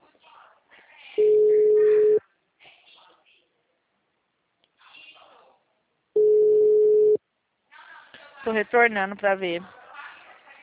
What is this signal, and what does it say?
Ringback tone (Europe, 425 Hz), with speech.